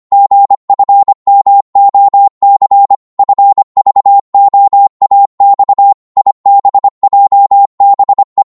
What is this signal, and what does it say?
Morse code, heard as a keyed tone, about 25 wpm, 800 Hz. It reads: GFMOC FVOAX IBJBE
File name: Morse / ltr-25wpm